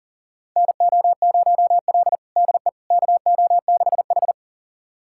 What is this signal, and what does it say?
Morse code, heard as a keyed tone, about 40 wpm, 700 Hz. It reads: NO0L DE KO6H
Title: USA Callsign DE Callsign - 4 Characters - 40wpm